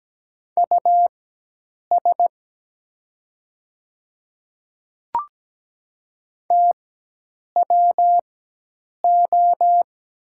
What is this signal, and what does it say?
Morse code, heard as a keyed tone, about 17 wpm, spaced as for 8 wpm, 700 Hz. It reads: US TWO
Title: 4x Character Spacing - Top 300 Words - 17wpm